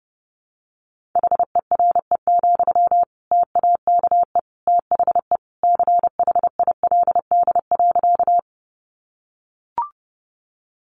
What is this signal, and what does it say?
Morse code, heard as a keyed tone, about 30 wpm, 700 Hz. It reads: HERE, TAKE THE CHILD.
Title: Sentences from Top 800 Words 30wpm